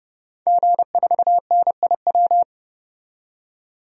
Morse code, heard as a keyed tone, about 31 wpm, 700 Hz. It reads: G4NIW
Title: DXCC-100 Call Signs - 5 Characters - 30wpm